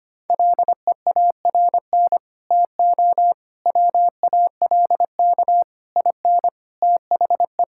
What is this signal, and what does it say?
Morse code, heard as a keyed tone, about 25 wpm, 700 Hz. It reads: LEARN TO WALK IN THE
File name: Sets of 9 Words - 25wpm